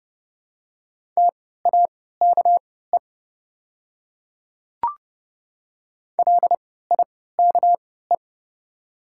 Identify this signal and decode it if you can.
Morse code, heard as a keyed tone, about 30 wpm, spaced as for 17 wpm, 690 Hz. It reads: TAKE LIKE
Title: 3x Character Spacing - Top 200 Words - 30wpm